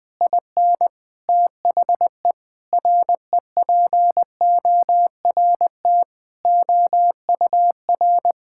Morse code, heard as a keyed tone, about 20 wpm, 700 Hz. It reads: IN THE REPORT OUR
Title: Sentences with 6-Word Phrases - Spoken Beginning - 20wpm